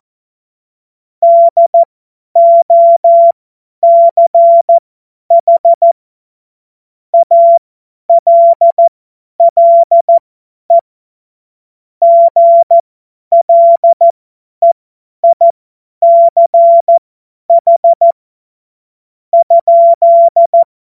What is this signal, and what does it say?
Morse code, heard as a keyed tone, about 14 wpm, spaced as for 9 wpm, 680 Hz. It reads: DOCH ALLE GLEICH?
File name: Morsekurs / Witze / Witz 01. Juni 2021, 50 F70